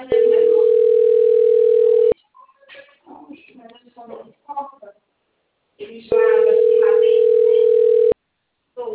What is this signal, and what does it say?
Ringback tone (North America), 440 + 480 Hz, with speech.
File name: out- - - - - .84358